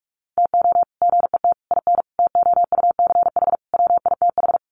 Morse code, heard as a keyed tone, about 44 wpm, 700 Hz. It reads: TO GET IN TOUCH WITH
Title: Sets of 5 Words Encore2 45wpm